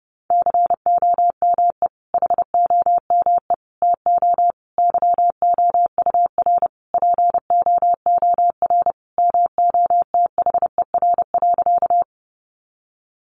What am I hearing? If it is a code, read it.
Morse code, heard as a keyed tone, about 30 wpm, 700 Hz. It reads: COME HOME TO YOUR POOR MOTHER.
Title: Sentences from Top 1000 Words 30wpm